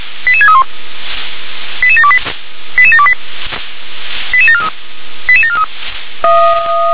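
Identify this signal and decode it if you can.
Five-tone selective call (ZVEI), heard as quick runs of five tones, tones of about 70 ms. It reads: 88631 88728 88728 88743 88743
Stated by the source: ZVEI five-tone selective call received on SDR.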